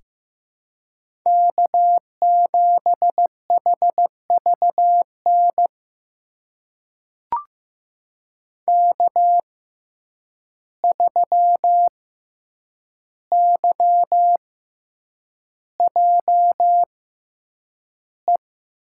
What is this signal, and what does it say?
Morse code, heard as a keyed tone, about 15 wpm, spaced as for 8 wpm, 700 Hz. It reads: K7HVN K3YJE
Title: USA Call Signs - 5 Characters - 6x Character Spacing - 15wpm